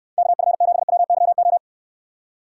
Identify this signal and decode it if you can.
Morse code, heard as a keyed tone, about 67 wpm, 700 Hz. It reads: KF7WYO